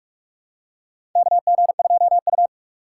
Morse code, heard as a keyed tone, about 46 wpm, 700 Hz. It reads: KG2U
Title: USA Call Signs - 4 Characters - 4x Character Spacing - 45wpm